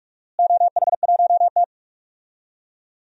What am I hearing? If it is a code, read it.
Morse code, heard as a keyed tone, about 46 wpm, 690 Hz. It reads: OH1T
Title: DXCC-100 Call Signs - 4 Characters - Repeated 3x - 45wpm